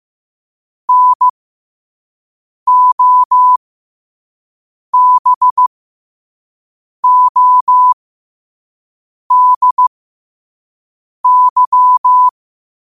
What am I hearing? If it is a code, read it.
Morse code, heard as a keyed tone, about 15 wpm, spaced as for 5 wpm, 1000 Hz. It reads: NOBODY
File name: qotdfiles / 5wpm